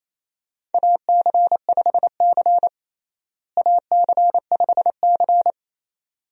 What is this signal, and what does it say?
Morse code, heard as a keyed tone, about 28 wpm, 700 Hz. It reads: AC5C AC5C
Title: USA Call Signs - 4 Characters - Repeated 2x - 28wpm